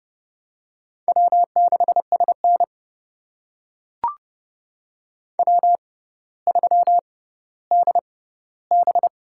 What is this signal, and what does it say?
Morse code, heard as a keyed tone, about 30 wpm, spaced as for 15 wpm, 700 Hz. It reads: W6SN W3DB
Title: USA Call Signs - 4 Characters - 6x Character Spacing - 30wpm